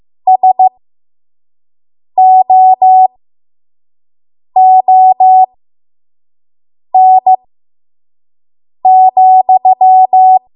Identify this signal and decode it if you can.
Morse code, heard as a keyed tone, about 15 wpm, spaced as for 5 wpm, 750 Hz. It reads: SOON,
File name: Archive / 5 WPM / 05WPM